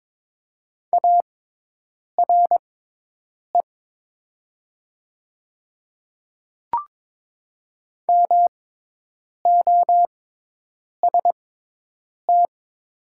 Morse code, heard as a keyed tone, about 22 wpm, spaced as for 8 wpm, 700 Hz. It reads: ARE MOST
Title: 6x Character Spacing - Top 100 Words - 22wpm